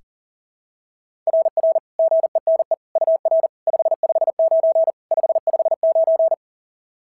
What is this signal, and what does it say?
Morse code, heard as a keyed tone, about 40 wpm, 690 Hz. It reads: RR GENE UR 559 559
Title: SOTA QSO Practice 40wpm